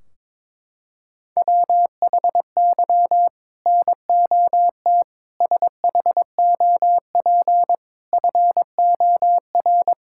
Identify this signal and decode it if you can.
Morse code, heard as a keyed tone, about 22 wpm, 700 Hz. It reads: WHY NOT SHOP FOR
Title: Sentences from Top 1600 Words 22wpm